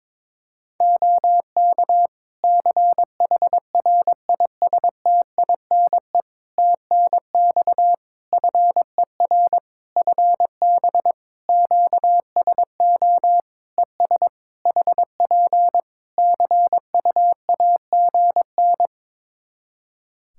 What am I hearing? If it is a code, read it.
Morse code, heard as a keyed tone, about 22 wpm, 700 Hz. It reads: OK CHRISTINE TNX FER FB QSO ES HP CUAGN
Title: Practice QSO - Exchange 4 - Ending - 22wpm